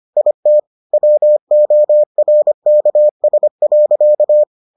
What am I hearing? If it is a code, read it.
Morse code, heard as a keyed tone, about 25 wpm, 600 Hz. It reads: IT WORKS.